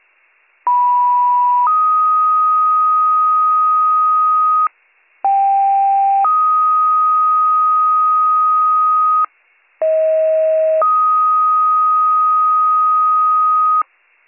Two-tone page, heard as a one-second tone, then a three-second tone, about 980 then 1250 Hz, about 790 then 1250 Hz, about 640 then 1250 Hz. It is heard from 0.7 to 13.8 s.